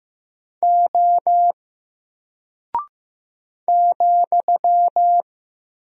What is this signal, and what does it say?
Morse code, heard as a keyed tone, about 15 wpm, 700 Hz. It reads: O ,
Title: Single Character (Letter:Number:Punctuation:Prosign) 15wpm